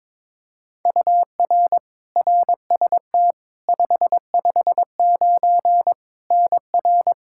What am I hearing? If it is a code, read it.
Morse code, heard as a keyed tone, about 22 wpm, 700 Hz. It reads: UR RST 559 NR
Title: Practice QSO - Exchange 1 - Beginning - 22wpm